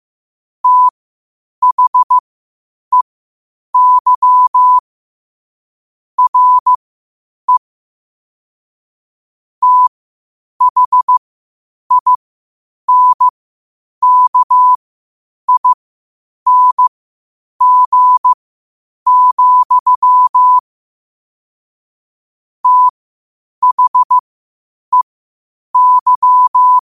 Morse code, heard as a keyed tone, about 15 wpm, spaced as for 8 wpm, 1000 Hz. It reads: THEY RE THINKING, THEY
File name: qotdfiles / 8wpm